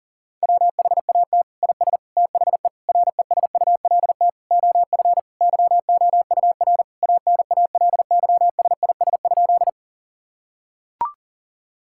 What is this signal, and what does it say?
Morse code, heard as a keyed tone, about 40 wpm, 700 Hz. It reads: WHAT IS THE RESULT OF YOUR ANALYSIS?
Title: Sentences from Top 1000 Words - Review Part 5 - 40wpm